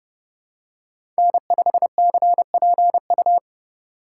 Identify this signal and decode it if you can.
Morse code, heard as a keyed tone, about 30 wpm, 700 Hz. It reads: N5CPU
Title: USA Call Signs - 8x Character Spacing - 30wpm